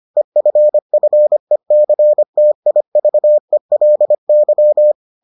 Morse code, heard as a keyed tone, about 25 wpm, 600 Hz. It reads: EFFECTIVELY